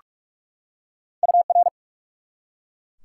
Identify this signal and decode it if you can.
Morse code, heard as a keyed tone, about 43 wpm, 700 Hz. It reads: UR
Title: Top 90 QSO Elements 45wpm